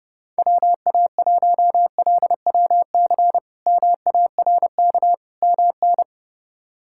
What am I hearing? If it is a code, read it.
Morse code, heard as a keyed tone, about 30 wpm, 700 Hz. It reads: WA1LWC MARK MN
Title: K1USN SST QSO Practice - 30wpm